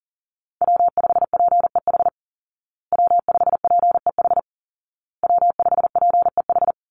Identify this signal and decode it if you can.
Morse code, heard as a keyed tone, about 40 wpm, 700 Hz. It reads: W5PEH W5PEH W5PEH
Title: Call Signs Repeated 2x - 40wpm